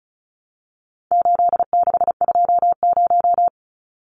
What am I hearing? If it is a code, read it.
Morse code, heard as a keyed tone, about 34 wpm, 700 Hz. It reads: 8620
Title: Numbers - 4-Digit 35wpm